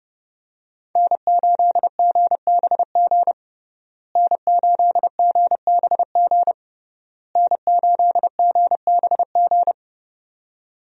Morse code, heard as a keyed tone, about 30 wpm, 700 Hz. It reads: N8GBG N8GBG N8GBG
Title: USA Call Signs - 5 Characters - Repeated 3x - 30wpm